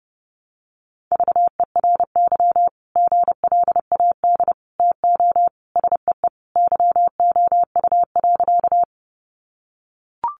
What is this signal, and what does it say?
Morse code, heard as a keyed tone, about 30 wpm, 700 Hz. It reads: VERY GLAD TO SEE YOU.
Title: Sentences from Top 800 Words 30wpm